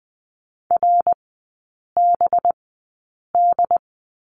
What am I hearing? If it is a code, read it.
Morse code, heard as a keyed tone, about 20 wpm, spaced as for 8 wpm, 690 Hz. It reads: RBD
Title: 191 Lesson 31e - Letter v - Three Characters 20wpm